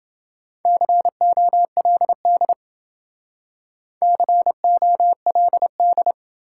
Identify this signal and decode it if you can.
Morse code, heard as a keyed tone, about 27 wpm, 700 Hz. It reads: COLD COLD